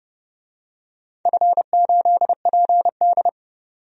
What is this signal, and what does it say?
Morse code, heard as a keyed tone, about 31 wpm, 700 Hz. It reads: F8PD
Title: DXCC-100 Call Signs - 4 Characters - 30wpm